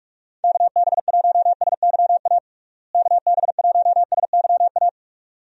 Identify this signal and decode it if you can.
Morse code, heard as a keyed tone, about 41 wpm, 700 Hz. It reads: KB1SYA KB1SYA
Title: USA Call Signs - 6 Characters - Repeated 2x - 45wpm